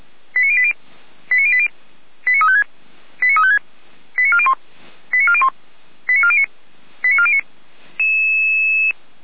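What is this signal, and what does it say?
Five-tone selective call (ZVEI), heard as quick runs of five tones, tones of about 70 ms. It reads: 89080 89080 89356 89356 89401 89401 89409 89409
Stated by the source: ZVEI five-tone call received with an SDR.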